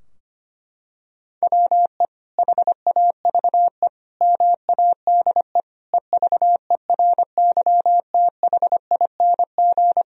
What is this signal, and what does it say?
Morse code, heard as a keyed tone, about 25 wpm, 700 Hz. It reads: WE HAVE MADE EVERYTHING
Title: Sentences from Top 1000 Words - Review Part 3 - 25wpm